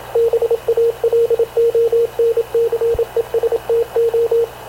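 Morse code, heard as a keyed tone, about 27 wpm, 460 Hz, with radio noise. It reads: BALONCESTO
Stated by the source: Havana Radio / CLA sending Spanish press (PX) in Morse, recorded by Radio Officer Paul Dumesnil onboard CNAV QUEST/CZDO.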